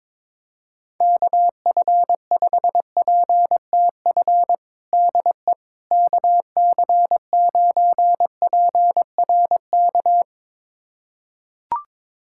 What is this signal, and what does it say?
Morse code, heard as a keyed tone, about 21 wpm, 700 Hz. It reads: KF5PTF DE KC9PRK
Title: USA Callsign DE Callsign - 6 Characters - 22wpm